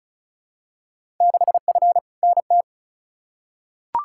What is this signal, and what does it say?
Morse code, heard as a keyed tone, about 36 wpm, 700 Hz. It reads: 6F NT
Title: Field Day Exchanges Encore - 35wpm